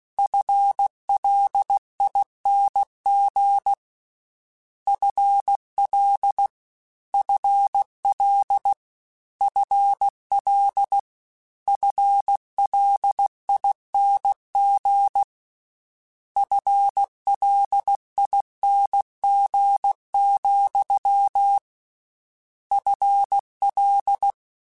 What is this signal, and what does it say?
Morse code, heard as a keyed tone, about 16 wpm, 800 Hz. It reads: FLING FL FL FL FLING FLING, FL
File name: Combinations / Combinations-mp3s / Combinations@15